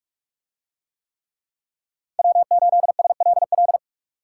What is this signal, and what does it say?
Morse code, heard as a keyed tone, about 40 wpm, 700 Hz. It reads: W8SLL